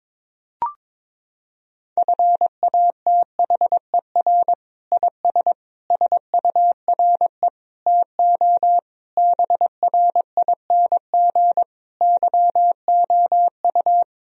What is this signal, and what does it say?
Morse code, heard as a keyed tone, about 22 wpm, 700 Hz. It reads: FATHER IS SURE TO BRING YOU
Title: Sentences from Top 1000 Words - Review Part 3 - 22wpm